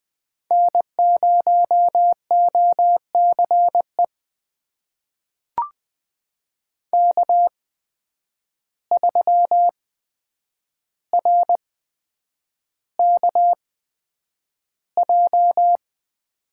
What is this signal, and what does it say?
Morse code, heard as a keyed tone, about 20 wpm, spaced as for 9 wpm, 700 Hz. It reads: N0OCE K3RKJ
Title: USA Call Signs - 5 Characters - 8x Character Spacing - 20wpm